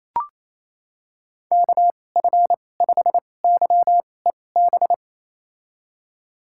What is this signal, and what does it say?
Morse code, heard as a keyed tone, about 28 wpm, spaced as for 20 wpm, 700 Hz. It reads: KF5YEB
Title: USA Call Signs - 6 Characters - 2x Character Spacing - 28wpm